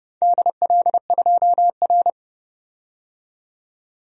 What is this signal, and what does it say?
Morse code, heard as a keyed tone, about 31 wpm, 700 Hz. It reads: DL2R
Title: DXCC-100 Call Signs - 4 Characters - 30wpm